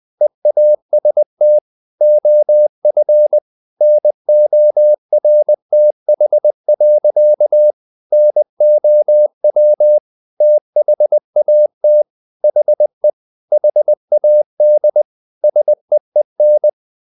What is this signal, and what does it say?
Morse code, heard as a keyed tone, about 20 wpm, 600 Hz. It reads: EAST OF NORTH. NOW THAT HE HAD SEEN
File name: the old man and the sea / the old man and the sea part 1 20wpm